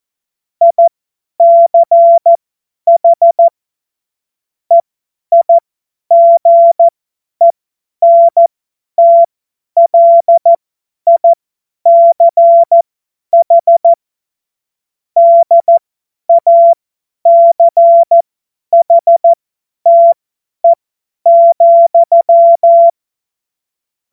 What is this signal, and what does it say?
Morse code, heard as a keyed tone, about 14 wpm, spaced as for 10 wpm, 680 Hz. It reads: ICH EIGENTLICH DACHTE,
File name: Morsekurs / Witze / Witz 23. August 2022 50 F70